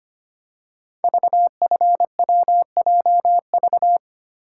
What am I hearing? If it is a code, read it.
Morse code, heard as a keyed tone, about 25 wpm, 700 Hz. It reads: VFWJV
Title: Code Groups - 5 Letters - 25wpm